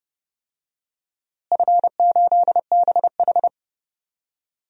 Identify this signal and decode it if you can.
Morse code, heard as a keyed tone, about 30 wpm, 700 Hz. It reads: F8BH
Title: DXCC-100 Call Signs - 4 Characters - Repeated 3x - 30wpm